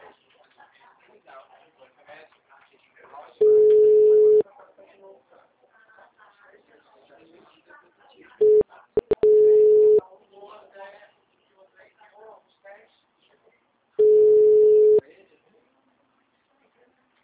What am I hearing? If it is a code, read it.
Ringback tone (Europe, 425 Hz), with speech.